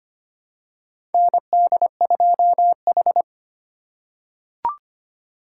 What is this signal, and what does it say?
Morse code, heard as a keyed tone, about 25 wpm, 700 Hz. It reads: ND2H
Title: USA Call Signs - 4 Characters - 3x Character Spacing - 25wpm